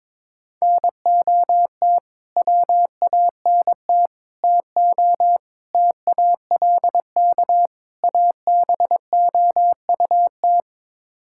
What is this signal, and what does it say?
Morse code, heard as a keyed tone, about 22 wpm, 700 Hz. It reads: NOT WANT TO TALK ABOUT
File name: practice / 22wpm / Build A Phrase - 5-Word Phrases - 22wpm